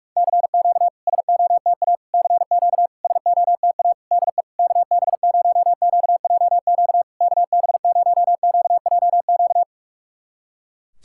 Morse code, heard as a keyed tone, about 45 wpm, 700 Hz. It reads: CQ SOTA CQ SOTA DE KB0QJQ KB0QJQ
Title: SOTA QSO Practice 45wpm